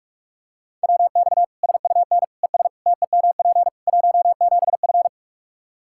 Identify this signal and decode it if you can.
Morse code, heard as a keyed tone, about 43 wpm, 700 Hz. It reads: WX SUN ES TEMP 17F
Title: K1USN SST QSO Practice 45wpm